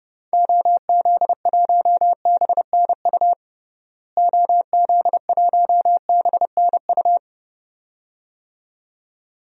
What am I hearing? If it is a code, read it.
Morse code, heard as a keyed tone, about 30 wpm, 700 Hz. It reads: OZ1BNU OZ1BNU
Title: DXCC-100 Call Signs - 6 Characters - Repeated 2x - 30wpm